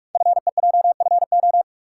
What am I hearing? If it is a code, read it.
Morse code, heard as a keyed tone, about 44 wpm, 700 Hz. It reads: FEJFO
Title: Code Groups - 5 Letters - 45wpm